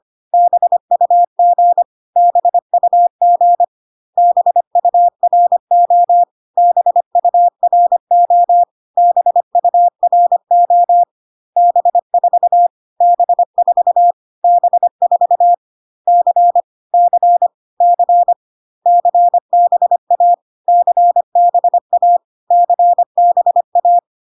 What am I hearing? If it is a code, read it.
Morse code, heard as a keyed tone, about 25 wpm, 700 Hz. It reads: BUG BUG BURO BURO BURO B4 B4 B4 C C C CBA CBA CBA
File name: CW abbreviations alphabetical repeated 3 times 25 25wpm ik1hge